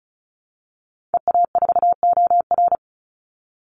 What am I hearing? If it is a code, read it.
Morse code, heard as a keyed tone, about 34 wpm, 700 Hz. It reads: EA4OR
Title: Call Signs - 35wpm